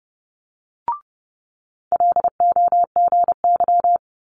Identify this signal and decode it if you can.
Morse code, heard as a keyed tone, about 30 wpm, 700 Hz. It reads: LOGY
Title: Multi-letter Phonemes Quick Fire 30wpm